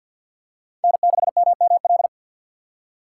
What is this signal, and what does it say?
Morse code, heard as a keyed tone, about 46 wpm, 700 Hz. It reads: N6MML
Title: USA Call Signs - 5 Characters - 2x Character Spacing - 50wpm